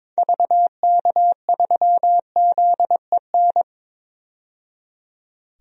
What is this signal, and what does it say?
Morse code, heard as a keyed tone, about 22 wpm, 700 Hz. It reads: VK3ZEN
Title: DXCC-100 Call Signs - 6 Characters - 22wpm